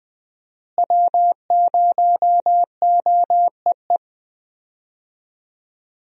Morse code, heard as a keyed tone, about 20 wpm, 700 Hz. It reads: W0OEE
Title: USA Call Signs - 5 Characters - Repeated 3x - 20wpm